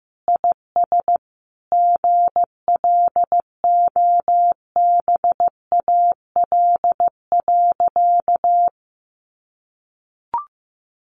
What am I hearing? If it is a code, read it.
Morse code, heard as a keyed tone, about 15 wpm, 700 Hz. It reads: IS GLOBAL.
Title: Sentences from Top 1200 Words 15wpm